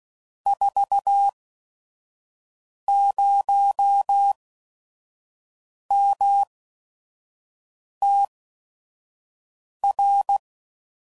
Morse code, heard as a keyed tone, about 16 wpm, spaced as for 5 wpm, 800 Hz. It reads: 40MTR
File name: CW Code Groups @5WPM / QSOs / kc7sac